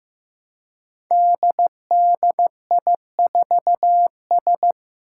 Morse code, heard as a keyed tone, about 15 wpm, 700 Hz. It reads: DDI4S
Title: Five Challenging Letter-Number Combinations 15wpm